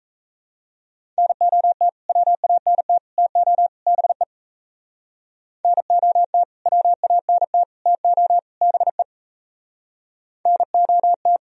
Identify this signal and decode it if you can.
Morse code, heard as a keyed tone, about 35 wpm, 700 Hz. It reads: NOT WANT TO BE NOT WANT TO BE NOT
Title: Speed-Racing - Sets of 4 Words 28wpm